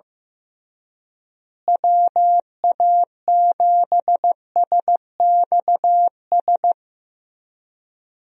Morse code, heard as a keyed tone, about 15 wpm, 700 Hz. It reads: WA7SXS
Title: USA Call Signs - 6 Characters - Repeated 3x - 15wpm